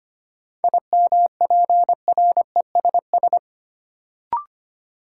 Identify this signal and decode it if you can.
Morse code, heard as a keyed tone, about 25 wpm, 700 Hz. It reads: IMPRESS
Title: Sentences from Top 1600 Words - New Word Drill - 25wpm